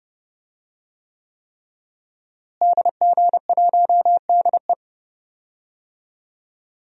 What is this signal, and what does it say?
Morse code, heard as a keyed tone, about 30 wpm, 700 Hz. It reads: DG1DE
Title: DXCC-100 Call Signs - 5 Characters - Repeated 3x - 30wpm